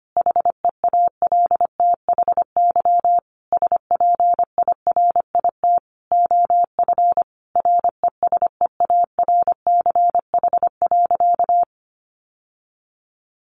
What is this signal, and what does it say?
Morse code, heard as a keyed tone, about 25 wpm, 700 Hz. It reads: HEALTHY SPIRIT OF RESEARCH.
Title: Sentences from Top 1400 Words 25wpm